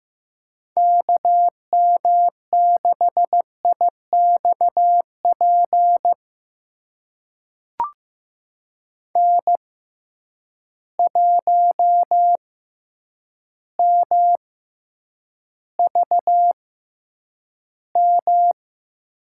Morse code, heard as a keyed tone, about 16 wpm, spaced as for 8 wpm, 700 Hz. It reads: KM6IXP N1MVM
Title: USA Call Signs - 6x Character Spacing - 15wpm